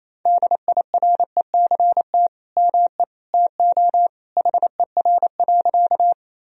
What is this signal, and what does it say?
Morse code, heard as a keyed tone, about 28 wpm, 700 Hz. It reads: DIRECT ME TO HER.
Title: Sentences from Top 1000 Words - Review Part 2 - 28wpm